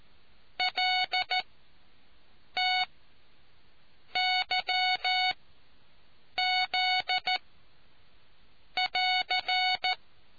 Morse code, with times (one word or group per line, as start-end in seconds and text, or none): 0.6-1.4 L
2.6-2.8 T
4.2-5.3 Y
6.4-7.4 Z
8.8-9.9 <AR>